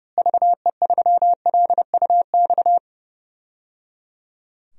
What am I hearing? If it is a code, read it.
Morse code, heard as a keyed tone, about 30 wpm, 700 Hz. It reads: VE3LUX
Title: DXCC-100 Call Signs - 6 Characters - Repeated 3x - 30wpm